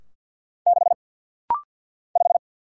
Morse code, heard as a keyed tone, about 49 wpm, 690 Hz. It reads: 6 5
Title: Single Letter-Number Alt - Rapid-Fire 50wpm